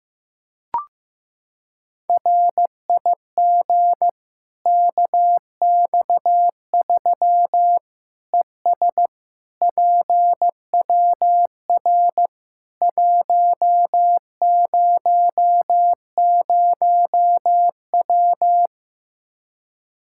Morse code, heard as a keyed tone, about 16 wpm, 700 Hz. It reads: RIG KX3 ES PWR 100W
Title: Practice QSO - Exchange 2 - Info - 15wpm